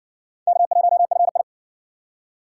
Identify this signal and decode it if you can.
Morse code, heard as a keyed tone, about 59 wpm, 700 Hz. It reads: K1FI